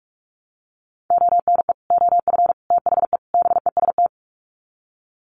Morse code, heard as a keyed tone, about 44 wpm, 700 Hz. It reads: ONE OF THE BEST
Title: Sets of 4 Words 45wpm